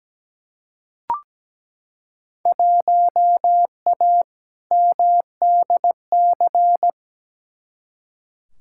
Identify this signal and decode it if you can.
Morse code, heard as a keyed tone, about 17 wpm, 700 Hz. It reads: 1A MDC
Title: Field Day Exchanges - 17wpm